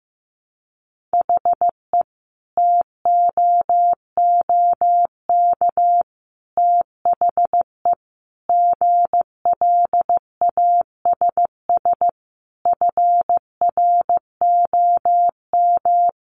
Morse code, heard as a keyed tone, about 15 wpm, 700 Hz. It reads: HE TOOK THE GLASS FROM
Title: Sentences from Top 1300 Words 15wpm